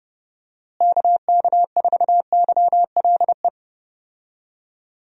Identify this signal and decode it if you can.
Morse code, heard as a keyed tone, about 32 wpm, 700 Hz. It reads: KK4YLE
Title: USA Call Signs - 6 Characters - Repeated 2x - 30wpm